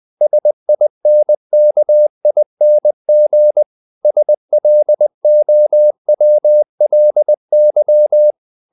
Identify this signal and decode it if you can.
Morse code, heard as a keyed tone, about 20 wpm, 600 Hz. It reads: SINKING SLOWLY
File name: the old man and the sea part 2 20wpm